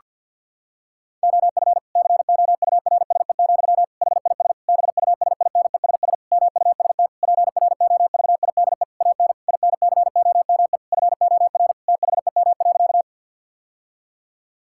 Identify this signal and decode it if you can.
Morse code, heard as a keyed tone, about 52 wpm, 700 Hz. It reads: OF COURSE, HIS BUSINESS MUST PROVIDE AN INCOME FOR THEM.
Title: Sentences from Top 1000 Words - Review Part 5 - 50wpm